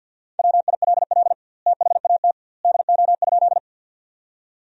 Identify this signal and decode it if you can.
Morse code, heard as a keyed tone, about 47 wpm, 700 Hz. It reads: WILL THAT DO?
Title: 2x Word-Spacing - Sentences from Top 100 Words - 50wpm